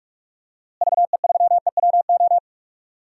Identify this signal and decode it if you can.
Morse code, heard as a keyed tone, about 46 wpm, 700 Hz. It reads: VE3EWO
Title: DXCC-100 Call Signs - 6 Characters - Repeated 3x - 45wpm